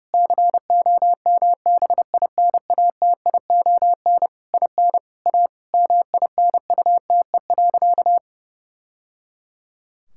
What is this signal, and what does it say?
Morse code, heard as a keyed tone, about 30 wpm, 700 Hz. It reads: COMBINATION IN A MINUTE.
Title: Sentences from Top 1000 Words - Review Part 6 - 30wpm